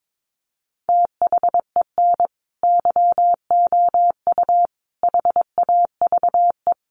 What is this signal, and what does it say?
Morse code, heard as a keyed tone, about 22 wpm, 700 Hz. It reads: THEN YOU HAVE
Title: Sentences from Top 1000 Words 22wpm